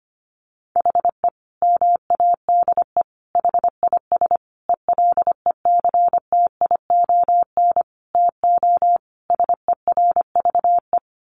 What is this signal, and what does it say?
Morse code, heard as a keyed tone, about 25 wpm, 700 Hz. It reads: HE MADE HIS ELECTION TO SERVE
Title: Sentences from Top 1500 Words 25wpm